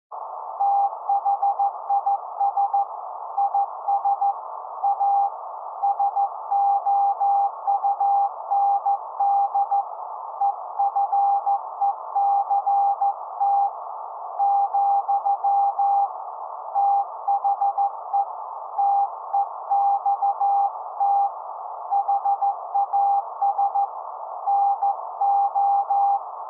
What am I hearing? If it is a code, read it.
Morse code, heard as a keyed tone, about 14 wpm, 800 Hz. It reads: THIS IS A SOUND EFECT , THE TEXT HAS NO
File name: se / pre2 / Morse Code02-4(Slow-Noise)